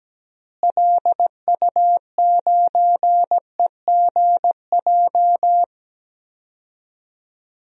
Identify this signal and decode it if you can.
Morse code, heard as a keyed tone, about 17 wpm, 700 Hz. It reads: LU9EGJ
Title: DXCC-100 Call Signs - 6 Characters - Repeated 2x - 17wpm